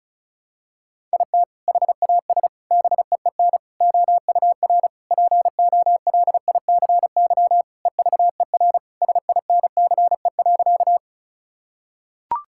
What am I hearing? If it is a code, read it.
Morse code, heard as a keyed tone, about 36 wpm, 700 Hz. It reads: IT HAS BEEN OUR POLICY EVER SINCE.
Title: Sentences from Top 1000 Words - Review Part 5 - 35wpm